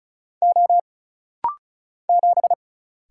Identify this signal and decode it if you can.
Morse code, heard as a keyed tone, about 34 wpm, 700 Hz. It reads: O 7
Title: Single Letter-Number Alt - Rapid-Fire 35wpm